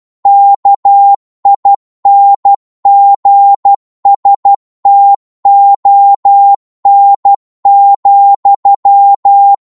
Morse code, heard as a keyed tone, about 12 wpm, 800 Hz. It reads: KINGSTON,
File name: Morse / qso-12wpm